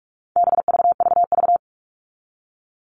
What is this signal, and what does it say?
Morse code, heard as a keyed tone, about 44 wpm, 700 Hz. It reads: BVVV
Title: All Cut Numbers - 4-Digit 45wpm